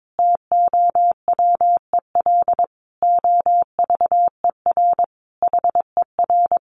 Morse code, heard as a keyed tone, about 22 wpm, 700 Hz. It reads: TOWEL OVER HER
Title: Easy Phrases 22wpm